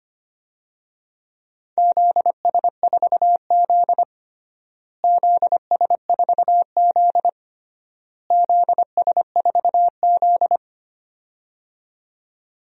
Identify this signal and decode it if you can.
Morse code, heard as a keyed tone, about 25 wpm, 700 Hz. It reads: ZS4Z ZS4Z ZS4Z
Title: DXCC-100 Call Signs - 4 Characters - Repeated 3x - 25wpm